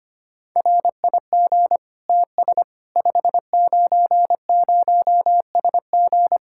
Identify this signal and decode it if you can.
Morse code, heard as a keyed tone, about 25 wpm, 700 Hz. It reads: RIG TS 590SG
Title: Practice QSO - Exchange 2 - Info - 25wpm